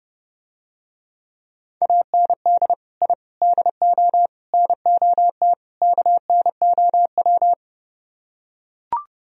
Morse code, heard as a keyed tone, about 30 wpm, 700 Hz. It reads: AND I DO NOT KNOW